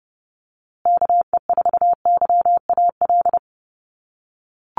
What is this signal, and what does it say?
Morse code, heard as a keyed tone, about 30 wpm, 700 Hz. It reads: KE4YAL
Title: USA Call Signs Repeated 3x 30wpm